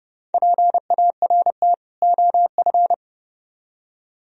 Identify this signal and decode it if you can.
Morse code, heard as a keyed tone, about 29 wpm, 700 Hz. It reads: PART OF